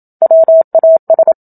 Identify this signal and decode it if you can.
Morse code, heard as a keyed tone, about 27 wpm, 650 Hz. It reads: WAS
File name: Book Two / 01-Chapter 1 - 27 WPM